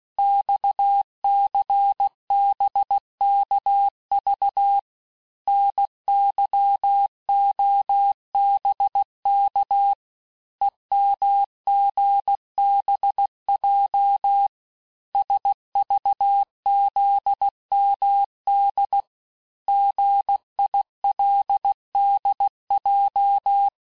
Morse code, heard as a keyed tone, about 16 wpm, 800 Hz. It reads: XCBKV NYOBK EMGBJ SVZMD GILDJ
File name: cours-du-lundi files / cours-1592--teste-chiffre-vitesse-15mpm